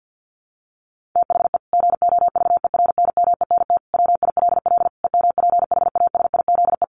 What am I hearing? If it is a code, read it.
Morse code, heard as a keyed tone, about 50 wpm, 700 Hz. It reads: THE GOVERNMENT WILL EMPHASIZE
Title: Sentences from Top 1500 Words 50wpm